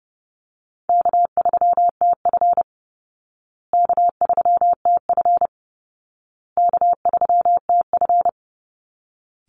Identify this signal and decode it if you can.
Morse code, heard as a keyed tone, about 30 wpm, 700 Hz. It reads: K3TF K3TF K3TF
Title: Easy Call Signs Repeated 3x - 30wpm